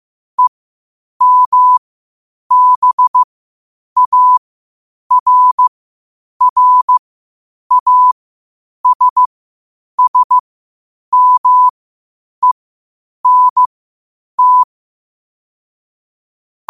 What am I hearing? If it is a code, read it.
Morse code, heard as a keyed tone, about 15 wpm, spaced as for 8 wpm, 1000 Hz. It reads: EMBARRASSMENT
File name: qotdfiles / 8wpm